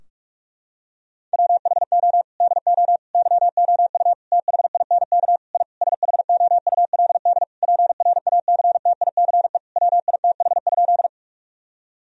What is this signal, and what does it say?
Morse code, heard as a keyed tone, about 46 wpm, 700 Hz. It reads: WHO DO YOU THINK I SHOULD PRACTICE WITH?
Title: Sentences from Top 1000 Words - Review Part 4 - 45wpm